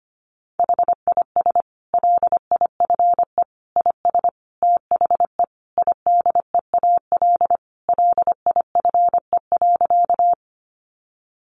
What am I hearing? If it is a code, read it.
Morse code, heard as a keyed tone, about 25 wpm, 700 Hz. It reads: HIS LIFE IS THE IDEAL LIFE.
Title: Sentences from Top 1400 Words 25wpm